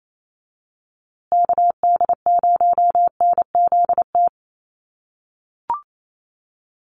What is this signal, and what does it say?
Morse code, heard as a keyed tone, about 28 wpm, 700 Hz. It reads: KD0NZT
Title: USA Call Signs Repeated 3x 28wpm